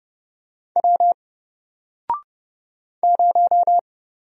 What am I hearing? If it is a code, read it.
Morse code, heard as a keyed tone, about 31 wpm, 700 Hz. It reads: W 0
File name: Single-Letter-Number-30wpm